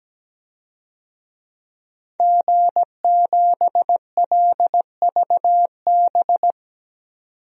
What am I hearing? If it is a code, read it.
Morse code, heard as a keyed tone, about 17 wpm, 700 Hz. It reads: G7LVB
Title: DXCC-100 Call Signs - 5 Characters - 17wpm